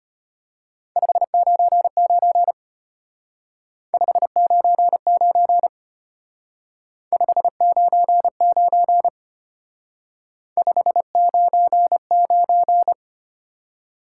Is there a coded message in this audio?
Yes: Morse code from 1.0 to 12.9 s, heard as a keyed tone, about 32 wpm, 700 Hz. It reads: 599 599 599 599